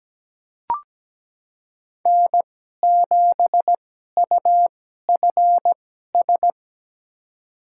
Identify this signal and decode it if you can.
Morse code, heard as a keyed tone, about 17 wpm, spaced as for 12 wpm, 700 Hz. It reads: N7UFS
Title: USA Call Signs - 5 Characters - 2x Character Spacing - 17wpm